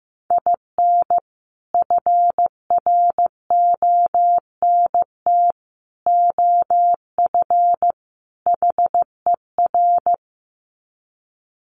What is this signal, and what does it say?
Morse code, heard as a keyed tone, about 15 wpm, 700 Hz. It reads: IN FRONT OF HER
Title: Sets of 4 Words 15wpm